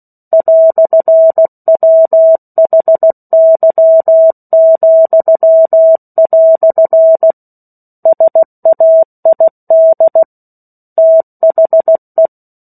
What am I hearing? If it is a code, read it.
Morse code, heard as a keyed tone, about 16 wpm, 650 Hz. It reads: "WHY," SAID THE
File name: 03-Chapter-03 16 WPM